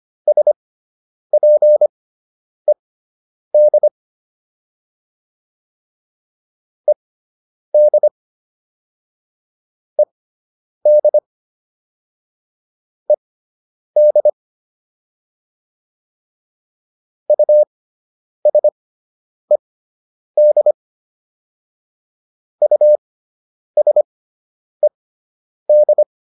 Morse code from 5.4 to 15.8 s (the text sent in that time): ED ED ED